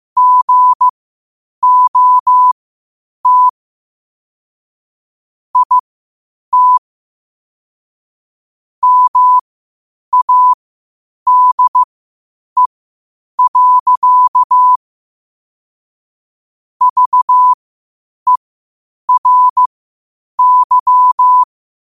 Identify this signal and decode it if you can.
Morse code, heard as a keyed tone, about 15 wpm, spaced as for 8 wpm, 1000 Hz. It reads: GOT IT MADE. VERY
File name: qotdfiles / 8wpm